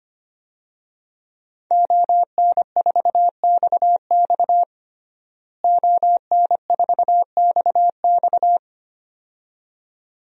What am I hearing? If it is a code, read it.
Morse code, heard as a keyed tone, about 25 wpm, 700 Hz. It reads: ON4XX ON4XX
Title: DXCC-100 Call Signs - 5 Characters - Repeated 2x - 25wpm